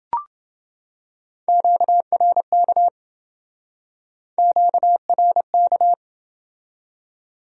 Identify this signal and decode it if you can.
Morse code, heard as a keyed tone, about 27 wpm, 700 Hz. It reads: QRK QRK